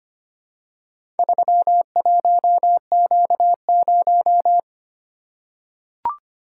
Morse code, heard as a keyed tone, about 25 wpm, 700 Hz. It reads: 31Q0
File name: practice / 25wpm / Four Letter-Number Combinations - 25wpm